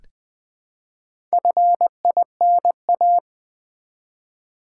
Morse code, heard as a keyed tone, about 20 wpm, 700 Hz. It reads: FINA
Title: Build A Word - 1 Letter At A Time - 5-Letter Words - 20wpm